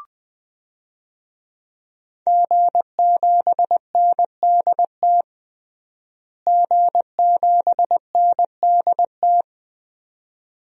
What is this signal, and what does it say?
Morse code, heard as a keyed tone, about 20 wpm, 700 Hz. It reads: G7NDT G7NDT
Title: DXCC-100 Call Signs - 5 Characters - Repeated 2x - 20wpm